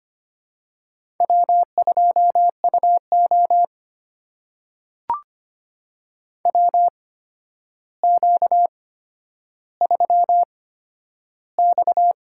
Morse code, heard as a keyed tone, about 25 wpm, spaced as for 11 wpm, 700 Hz. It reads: W2UO WQ3X
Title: USA Call Signs - 4 Characters - 8x Character Spacing - 25wpm